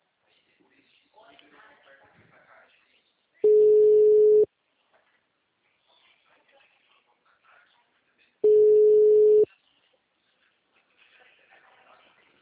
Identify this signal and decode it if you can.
Ringback tone (Europe, 425 Hz), with speech.
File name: out- -2011- - - .32044